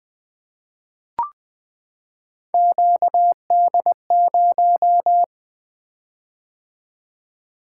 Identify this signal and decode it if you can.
Morse code, heard as a keyed tone, about 20 wpm, 700 Hz. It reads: QD0
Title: Three Letter-Number Combinations - 20wpm